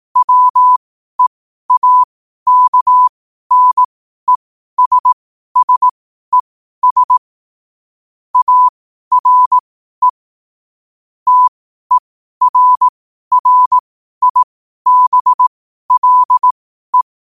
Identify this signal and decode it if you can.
Morse code, heard as a keyed tone, about 18 wpm, spaced as for 12 wpm, 1000 Hz. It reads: WEAKNESSES ARE TERRIBLE
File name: qotdfiles / 12wpm